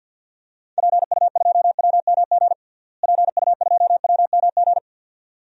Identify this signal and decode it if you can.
Morse code, heard as a keyed tone, about 53 wpm, 700 Hz. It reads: PU2WMG PU2WMG
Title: DXCC-100 Call Signs - 6 Characters - Repeated 2x - 50wpm